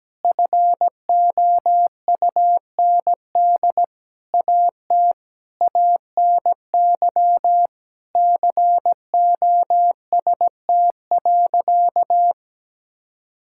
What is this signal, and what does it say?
Morse code, heard as a keyed tone, about 17 wpm, 700 Hz. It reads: FOUND AT ANY COST.
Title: Sentences from Top 1000 Words - Review Part 5 - 17wpm